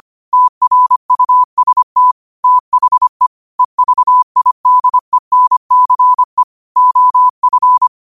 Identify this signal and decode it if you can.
Morse code, heard as a keyed tone, about 25 wpm, 1000 Hz. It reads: TRUST THE EVIDENCE OF